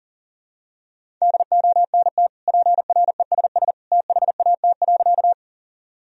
Morse code, heard as a keyed tone, about 40 wpm, 700 Hz. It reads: DONT PRESS THAT.
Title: Sentences from Top 1000 Words - Review Part 1 - 40wpm